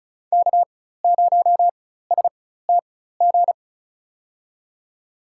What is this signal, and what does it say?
Morse code, heard as a keyed tone, about 36 wpm, spaced as for 16 wpm, 700 Hz. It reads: K0STG
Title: USA Call Signs - 5 Characters - 4x Character Spacing - 35wpm